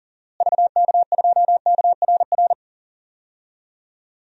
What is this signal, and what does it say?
Morse code, heard as a keyed tone, about 40 wpm, 700 Hz. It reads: VK2KRR
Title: DXCC-100 Call Signs - 6 Characters - Repeated 2x - 40wpm